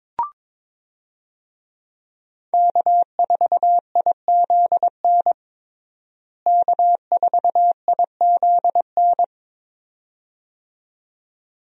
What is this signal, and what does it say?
Morse code, heard as a keyed tone, about 21 wpm, 700 Hz. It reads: K4IZN K4IZN
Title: USA Call Signs - 5 Characters - Repeated 2x - 22wpm